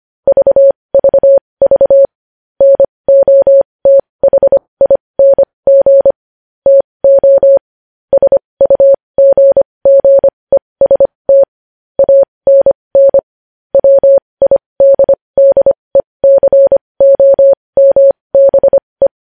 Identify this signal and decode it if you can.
Morse code, heard as a keyed tone, about 25 wpm, spaced as for 20 wpm, 550 Hz. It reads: VVV NOTHING TO SUGGEST ANN WIDDECOMBE
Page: news-headlines-in-morse-code-at-20-wpm